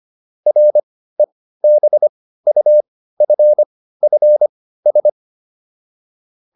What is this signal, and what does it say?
Morse code, heard as a keyed tone, about 25 wpm, spaced as for 15 wpm, 600 Hz. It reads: REBUFFS